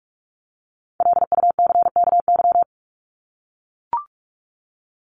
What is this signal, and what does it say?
Morse code, heard as a keyed tone, about 44 wpm, 700 Hz. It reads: LUCKY
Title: Top 1751-2000 Words 45wpm